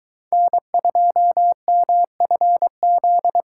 Morse code, heard as a keyed tone, about 22 wpm, 700 Hz. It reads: N2MFZ